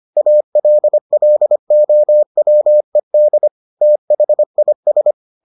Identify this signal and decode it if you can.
Morse code, heard as a keyed tone, about 25 wpm, 600 Hz. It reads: ALLOWED THIS